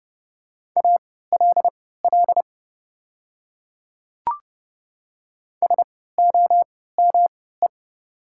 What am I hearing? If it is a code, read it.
Morse code, heard as a keyed tone, about 30 wpm, spaced as for 17 wpm, 690 Hz. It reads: ALL SOME
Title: 3x Character Spacing - Top 500 Words - 30wpm